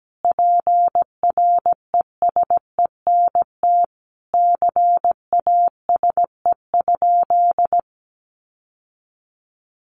Morse code, heard as a keyed tone, about 17 wpm, 700 Hz. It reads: PRESENT CASE?
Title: Sentences from Top 1200 Words 17wpm